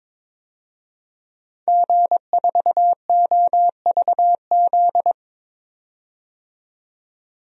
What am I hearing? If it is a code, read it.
Morse code, heard as a keyed tone, about 22 wpm, 700 Hz. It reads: G4OVZ
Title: DXCC-100 Call Signs - 5 Characters - 22wpm